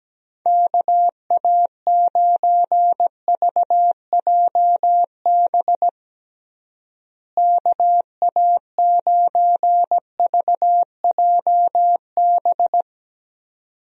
Morse code, heard as a keyed tone, about 17 wpm, 700 Hz. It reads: KA9VJB KA9VJB
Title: USA Call Signs - 6 Characters - Repeated 2x - 17wpm